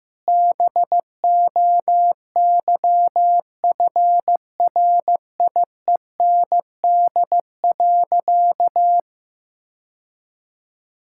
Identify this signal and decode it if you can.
Morse code, heard as a keyed tone, about 15 wpm, 700 Hz. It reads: BOYFRIEND.
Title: Sentences from Top 1000 Words - Review Part 6 - 15wpm